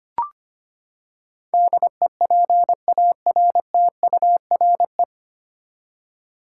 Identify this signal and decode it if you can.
Morse code, heard as a keyed tone, about 25 wpm, 700 Hz. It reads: DEPARTURE
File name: Nine Letter Words - 25wpm